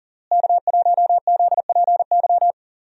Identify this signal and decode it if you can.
Morse code, heard as a keyed tone, about 40 wpm, 700 Hz. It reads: K1ZPY
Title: USA Call Signs - 5 Characters - Repeated 3x - 40wpm